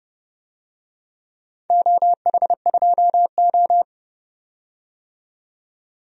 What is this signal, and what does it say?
Morse code, heard as a keyed tone, about 30 wpm, 700 Hz. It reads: OH2O
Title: DXCC-100 Call Signs - 4 Characters - 30wpm